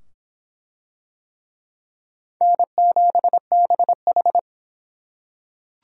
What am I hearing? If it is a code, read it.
Morse code, heard as a keyed tone, about 23 wpm, 700 Hz. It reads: N7BH